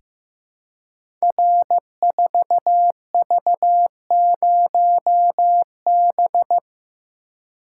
Morse code, heard as a keyed tone, about 15 wpm, 700 Hz. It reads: R4V0B
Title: Code Groups - 5 Letter:Numbers - 15wpm